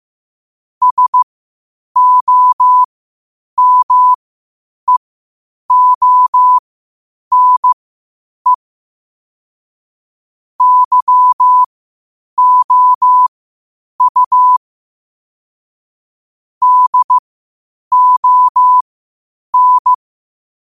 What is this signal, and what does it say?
Morse code, heard as a keyed tone, about 15 wpm, spaced as for 8 wpm, 1000 Hz. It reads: SOMEONE YOU DON
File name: qotdfiles / 8wpm